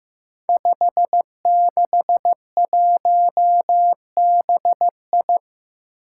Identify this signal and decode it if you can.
Morse code, heard as a keyed tone, about 15 wpm, 700 Hz. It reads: 561BI
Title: Five Challenging Letter-Number Combinations 15wpm